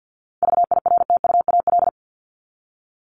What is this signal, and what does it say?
Morse code, heard as a keyed tone, about 50 wpm, 700 Hz. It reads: VIRTUAL
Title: LY Suffix - Top 100 Words with Base Words 50wpm